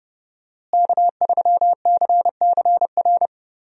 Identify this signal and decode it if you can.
Morse code, heard as a keyed tone, about 30 wpm, 700 Hz. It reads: K3CCR
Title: USA Call Signs - 5 Characters - Repeated 3x - 30wpm